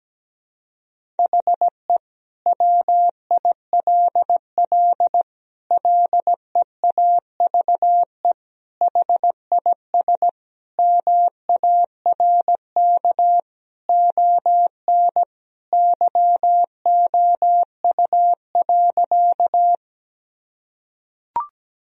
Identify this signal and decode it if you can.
Morse code, heard as a keyed tone, about 17 wpm, 700 Hz. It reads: HE WILL LEAVE HIS MARK ON YOU.
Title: Sentences from Top 1000 Words - Review Part 1 - 17wpm